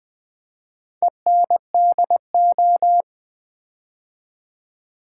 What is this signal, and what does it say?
Morse code, heard as a keyed tone, about 20 wpm, 700 Hz. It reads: ENDO
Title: Build A Word - 4 Letters At A Time - 11-Letter Words - 20wpm